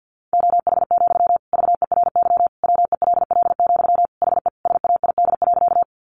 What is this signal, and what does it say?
Morse code, heard as a keyed tone, about 50 wpm, 700 Hz. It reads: OH, VERY WELL, HE SAID.
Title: Sentences from Top 500 Words 50wpm